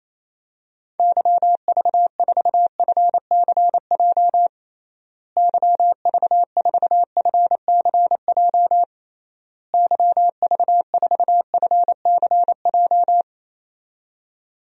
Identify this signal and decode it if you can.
Morse code, heard as a keyed tone, about 28 wpm, 700 Hz. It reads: YV4FCJ YV4FCJ YV4FCJ
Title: DXCC-100 Call Signs - 6 Characters - Repeated 3x - 28wpm